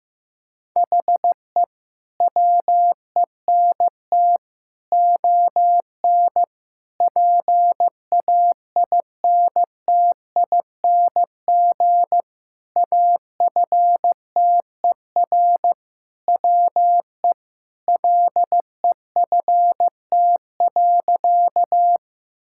Morse code, heard as a keyed tone, about 16 wpm, 700 Hz. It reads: HE WENT ON PAINTING AFTER WE LEFT.
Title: Sentences from Top 1000 Words - Review Part 6 - 15wpm